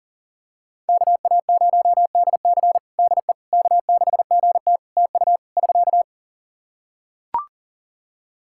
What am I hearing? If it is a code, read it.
Morse code, heard as a keyed tone, about 40 wpm, 700 Hz. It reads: KA0DC DE K6GT TU <SK>
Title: Practice QSO - Exchange 4 - Ending - 40wpm